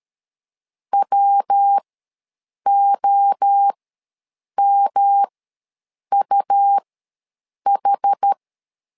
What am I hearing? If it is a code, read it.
Morse code, heard as a keyed tone, about 13 wpm, spaced as for 7 wpm, 800 Hz. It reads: WOMUH
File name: LEZIONI / 3 MWRO / 3 MWRO ws 12wpm 1914